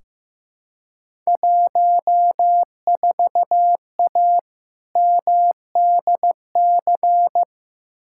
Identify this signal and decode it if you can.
Morse code, heard as a keyed tone, about 15 wpm, 700 Hz. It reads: 14A MDC
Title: Field Day Exchanges Encore - 15wpm